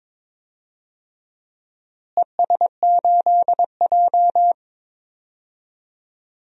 Morse code, heard as a keyed tone, about 22 wpm, 700 Hz. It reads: ES8J
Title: DXCC-100 Call Signs - 4 Characters - Repeated 2x - 22wpm